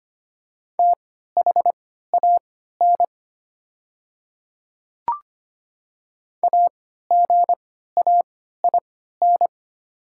Morse code, heard as a keyed tone, about 25 wpm, spaced as for 14 wpm, 690 Hz. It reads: THAN AGAIN
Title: 3x Character Spacing - Top 200 Words - 25wpm